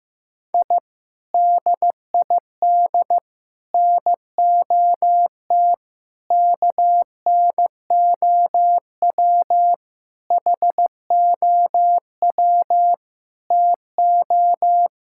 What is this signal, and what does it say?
Morse code, heard as a keyed tone, about 15 wpm, 700 Hz. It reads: I DID NOT KNOW HOW TO
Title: Sentences from Top 1000 Words - Review Part 6 - 15wpm